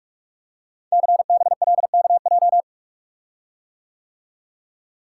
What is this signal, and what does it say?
Morse code, heard as a keyed tone, about 44 wpm, 700 Hz. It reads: CBLKJ
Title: Code Groups - 5 Letters - 45wpm